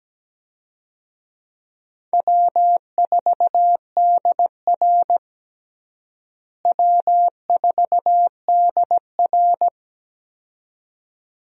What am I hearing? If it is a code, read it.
Morse code, heard as a keyed tone, about 17 wpm, 700 Hz. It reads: W4DR W4DR
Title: USA Call Signs - 4 Characters - Repeated 2x - 17wpm